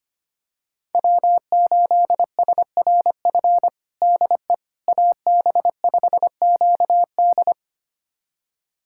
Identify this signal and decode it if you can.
Morse code, heard as a keyed tone, about 25 wpm, 700 Hz. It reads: W8SRF DE AB5QD
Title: USA Callsign DE Callsign - 5 Characters - 25wpm